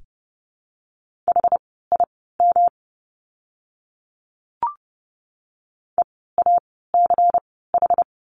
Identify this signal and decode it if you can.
Morse code, heard as a keyed tone, about 30 wpm, spaced as for 17 wpm, 700 Hz. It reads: HIM EACH
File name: course / 30:10wpm / 230 Lesson 37f - Letter x - Words 30wpm